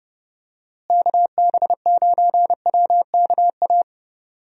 Morse code, heard as a keyed tone, about 30 wpm, 700 Hz. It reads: KB9WKA
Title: USA Call Signs - 6 Characters - Repeated 3x - 30wpm